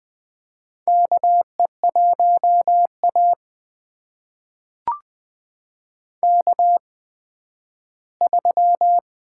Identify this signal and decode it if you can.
Morse code, heard as a keyed tone, about 20 wpm, spaced as for 12 wpm, 700 Hz. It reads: KE1A K3
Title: USA Call Signs - 4 Characters - 8x Character Spacing - 20wpm